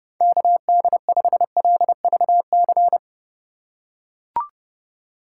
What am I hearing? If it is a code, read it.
Morse code, heard as a keyed tone, about 32 wpm, 700 Hz. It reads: KD5LVC